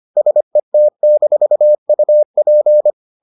Morse code, heard as a keyed tone, about 25 wpm, 600 Hz. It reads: SET-UP